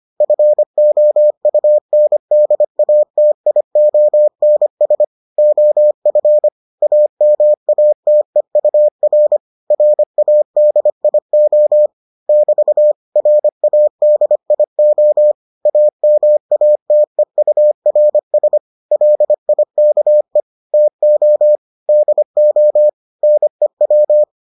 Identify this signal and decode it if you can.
Morse code, heard as a keyed tone, about 25 wpm, 600 Hz. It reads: FOUNDATIONS OF AMATEUR RADIO <BT> RADIO AMATEURS LIKE TO DO NEW